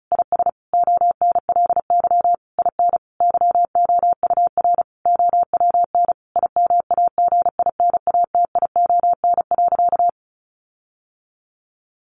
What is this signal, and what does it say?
Morse code, heard as a keyed tone, about 34 wpm, 700 Hz. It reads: IS ONLY IN YOUR OWN IMAGINATION.
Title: Sentences from Top 1300 Words 35wpm